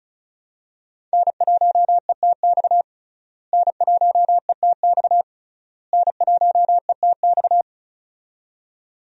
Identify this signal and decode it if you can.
Morse code, heard as a keyed tone, about 36 wpm, 700 Hz. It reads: N1ETX N1ETX N1ETX
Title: USA Call Signs - 5 Characters - Repeated 3x - 35wpm